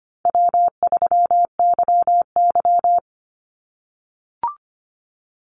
Morse code, heard as a keyed tone, about 25 wpm, 700 Hz. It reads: W3YY
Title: Easy Call Signs - 25wpm